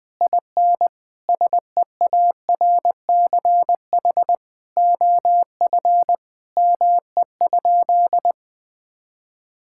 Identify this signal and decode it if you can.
Morse code, heard as a keyed tone, about 20 wpm, 700 Hz. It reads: IN SEARCH OF ME?
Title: Sentences from Top 1000 Words - Review Part 5 - 20wpm